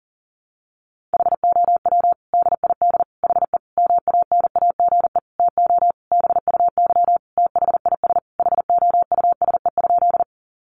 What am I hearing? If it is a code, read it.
Morse code, heard as a keyed tone, about 40 wpm, 700 Hz. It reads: HOW DID HE MANAGE TO BUY THIS HOUSE?
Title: Sentences from Top 1400 Words 40wpm